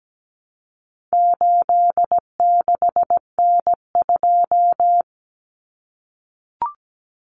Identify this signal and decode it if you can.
Morse code, heard as a keyed tone, about 17 wpm, 700 Hz. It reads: 86N2
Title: Common Cut Numbers - 4-Digit 17wpm